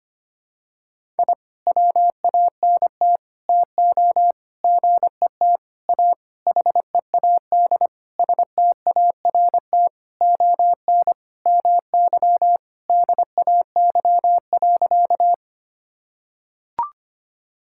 Morse code, heard as a keyed tone, about 25 wpm, 700 Hz. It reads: I WANT TO GET A HEAD START ON MY DAY.
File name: practice / 25wpm / Sentences with 9-Word Phrases - 25wpm